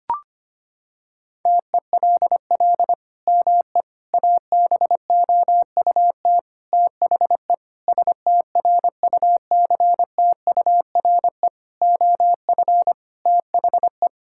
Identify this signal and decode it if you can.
Morse code, heard as a keyed tone, about 25 wpm, 700 Hz. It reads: TELL ME ABOUT THE STRUCTURE OF THE
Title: Sentences from Top 1000 Words - Review Part 6 - 25wpm